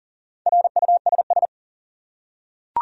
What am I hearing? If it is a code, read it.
Morse code, heard as a keyed tone, about 40 wpm, 700 Hz. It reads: RUSS
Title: USA Common Names - Max 5 Letters - 40wpm